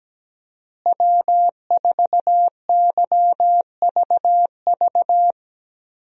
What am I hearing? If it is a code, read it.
Morse code, heard as a keyed tone, about 17 wpm, 700 Hz. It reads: W4YVV
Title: USA Call Signs - 5 Characters - Repeated 2x - 17wpm